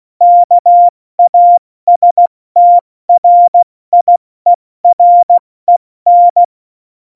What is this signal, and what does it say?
Morse code, heard as a keyed tone, about 16 wpm, 700 Hz. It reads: KASTRIEREN
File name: Morsekurs / Witze / Witz 07 April 2020, 70 F80